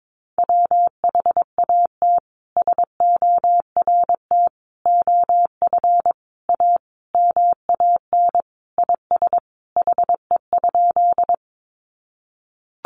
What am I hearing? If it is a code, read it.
Morse code, heard as a keyed tone, about 22 wpm, 700 Hz. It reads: WHAT SORT OF A MAN IS HE?
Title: Sentences from Top 1300 Words 22wpm